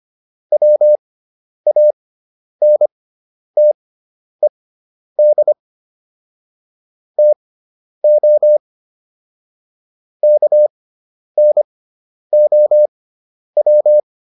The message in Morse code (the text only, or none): WANTED TO KNOW